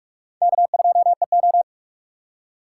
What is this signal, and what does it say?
Morse code, heard as a keyed tone, about 41 wpm, 700 Hz. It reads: K2EO